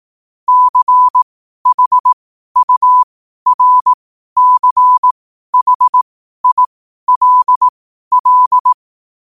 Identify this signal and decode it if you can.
Morse code, heard as a keyed tone, about 18 wpm, spaced as for 12 wpm, 1000 Hz. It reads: CHURCHILL
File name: qotdfiles / 12wpm